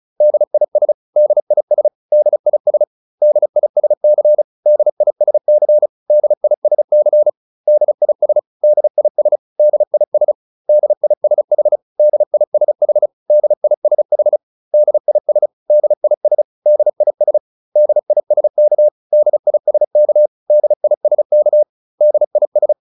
Morse code, heard as a keyed tone, about 34 wpm, 600 Hz. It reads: DIS DIS DIS DISC DISC DISC DIS DIS DIS DISH DISH DISH DIS DIS DIS DISK DISK DISK DIS